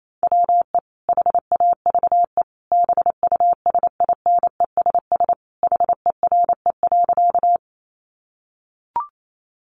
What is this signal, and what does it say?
Morse code, heard as a keyed tone, about 28 wpm, 700 Hz. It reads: WE HAVE BUSINESS HERE.
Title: Sentences from Top 600 Words 28wpm